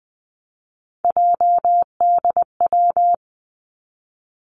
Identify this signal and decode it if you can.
Morse code, heard as a keyed tone, about 20 wpm, 700 Hz. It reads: JDW
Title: Three Challenging Letter-Number Combinations 20wpm